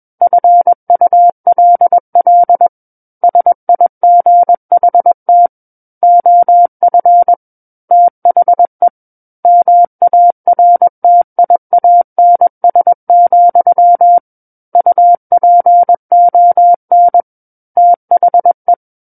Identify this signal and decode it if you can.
Morse code, heard as a keyed tone, about 21 wpm, 700 Hz. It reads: FULL SIGHT OF THE MARTIANS, UPON THE
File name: Book One / 12-Chapter 12 - 21 WPM